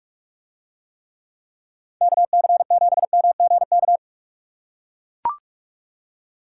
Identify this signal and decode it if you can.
Morse code, heard as a keyed tone, about 41 wpm, 700 Hz. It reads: KC7MGK